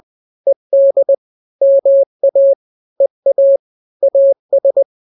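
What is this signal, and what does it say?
Morse code, heard as a keyed tone, about 20 wpm, 550 Hz. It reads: ED MA EA AS
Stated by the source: Morse drill of two-letter n-grams at 20/18 WPM (Farnsworth), from KK7RR.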